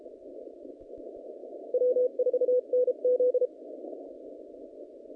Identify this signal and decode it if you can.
Morse code, heard as a keyed tone, about 33 wpm, 500 Hz. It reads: W4NZ
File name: contests / naqp-cw-k5tr-op-k5pi- -0100